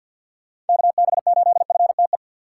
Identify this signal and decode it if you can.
Morse code, heard as a keyed tone, about 48 wpm, 700 Hz. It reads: KB8FTE
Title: USA Call Signs - 6 Characters - 3x Character Spacing - 50wpm